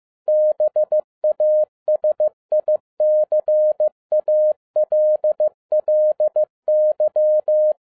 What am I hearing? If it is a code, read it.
Morse code, heard as a keyed tone, about 15 wpm, 610 Hz. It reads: BASICALLY
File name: 2011-07-mp3 / n22 QRP OTAKU 15wpm